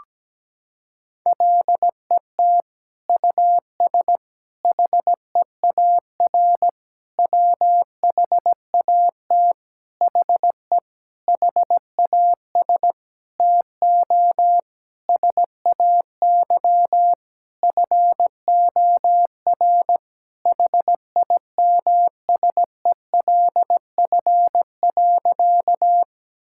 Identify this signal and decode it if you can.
Morse code, heard as a keyed tone, about 17 wpm, 700 Hz. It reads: LET US HEAR WHAT HE HAS TO SAY FOR HIMSELF.
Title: Sentences from Top 1600 Words 17wpm